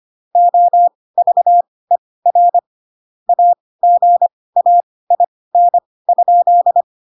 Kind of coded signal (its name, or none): Morse code